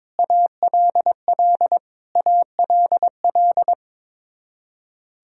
Morse code, heard as a keyed tone, about 21 wpm, 700 Hz. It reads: ALL ALL
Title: Top 100 Words Repeated 22wpm